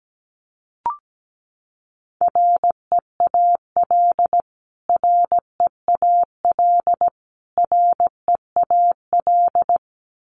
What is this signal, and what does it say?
Morse code, heard as a keyed tone, about 17 wpm, 700 Hz. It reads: REAL REAL REAL
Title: Top 500 Words Repeated 17wpm